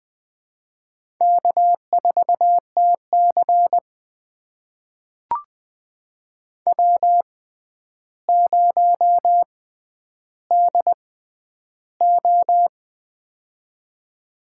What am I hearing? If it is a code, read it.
Morse code, heard as a keyed tone, about 21 wpm, spaced as for 10 wpm, 700 Hz. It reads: K4TC W0DO
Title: USA Call Signs - 4 Characters - 6x Character Spacing - 20wpm